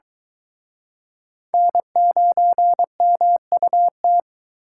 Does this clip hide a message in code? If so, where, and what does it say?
Yes: Morse code from 1.5 to 4.2 s, heard as a keyed tone, about 24 wpm, 700 Hz. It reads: N9MUT